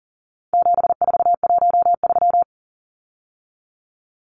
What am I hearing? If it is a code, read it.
Morse code, heard as a keyed tone, about 40 wpm, 700 Hz. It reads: 7413
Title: Numbers - 4-Digit 40wpm